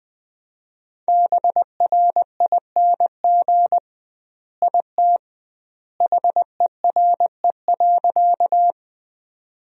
Morse code, heard as a keyed tone, about 21 wpm, 700 Hz. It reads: BRING IT HERE.
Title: 2x Word-Spacing - Sentences from Top 500 Words - 20wpm